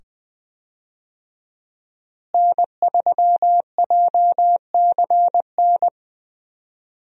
Morse code, heard as a keyed tone, about 20 wpm, 700 Hz. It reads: N3JCN
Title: USA Call Signs - 5 Characters - Repeated 2x - 20wpm